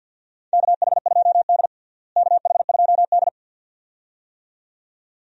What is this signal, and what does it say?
Morse code, heard as a keyed tone, about 52 wpm, 700 Hz. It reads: KH2D KH2D
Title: USA Call Signs - 4 Characters - Repeated 2x - 50wpm